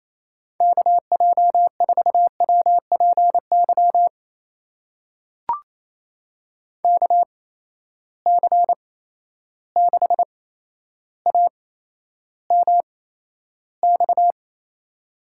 Morse code, heard as a keyed tone, about 28 wpm, spaced as for 12 wpm, 700 Hz. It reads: KJ4WPY KC6AMX
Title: USA Call Signs - 8x Character Spacing - 28wpm